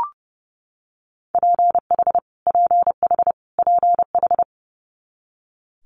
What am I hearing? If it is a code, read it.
Morse code, heard as a keyed tone, about 30 wpm, 700 Hz. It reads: PH PH PH
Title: Multi-letter Phonemes Repeated 3x 30wpm